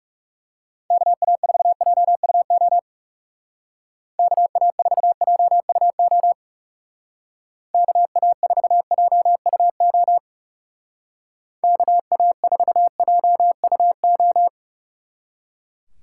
Morse code, heard as a keyed tone, about 37 wpm, 700 Hz. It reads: KA4JUO KA4JUO KA4JUO KA4JUO
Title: Speed Racing - USA 2x3 Call Signs 30wpm